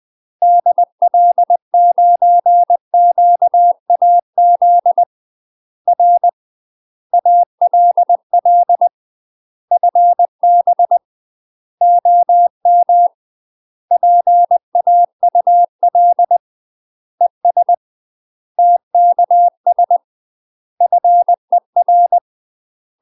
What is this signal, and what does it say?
Morse code, heard as a keyed tone, about 20 wpm, 700 Hz. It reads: DL9QAZ R ALL FB OM PAUL ES TKS FER
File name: UK MASTER QSOs TRAINING / 20WPM / UK MASTER QSOs 0017